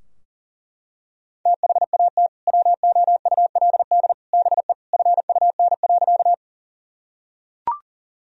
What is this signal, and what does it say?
Morse code, heard as a keyed tone, about 40 wpm, 700 Hz. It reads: THAT WOULD BE FUN.
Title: Sentences from Top 1000 Words - Review Part 4 - 40wpm